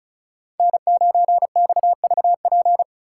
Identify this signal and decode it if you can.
Morse code, heard as a keyed tone, about 47 wpm, spaced as for 41 wpm, 700 Hz. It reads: N9XVP
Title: Speed Racing - USA Call Signs 35wpm